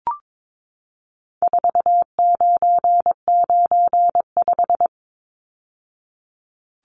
Morse code, heard as a keyed tone, about 22 wpm, 700 Hz. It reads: 4995
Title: Numbers - 4-Digit 22wpm